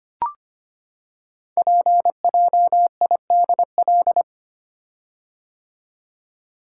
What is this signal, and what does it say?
Morse code, heard as a keyed tone, about 25 wpm, 700 Hz. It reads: PJIDL
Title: Code Groups - 5 Letters - 25wpm